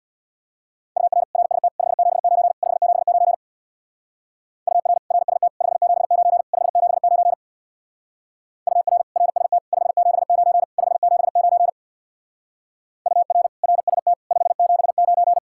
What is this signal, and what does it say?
Morse code, heard as a keyed tone, about 66 wpm, 700 Hz. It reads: UR RST 579 579 UR RST 579 579 UR RST 579 579 UR RST 579